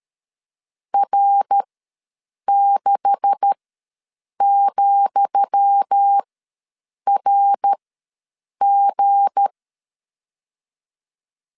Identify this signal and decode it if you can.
Morse code, heard as a keyed tone, about 13 wpm, spaced as for 7 wpm, 800 Hz. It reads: R6,RG
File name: LEZIONI / 10 SIGNS / 10D COMA ws 12wpm 3618